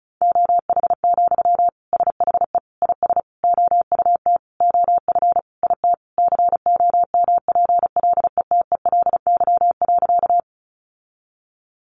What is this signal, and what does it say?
Morse code, heard as a keyed tone, about 34 wpm, 700 Hz. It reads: OH, SHE IS OUT OF IT COMPLETELY.
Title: Sentences from Top 1100 Words 35wpm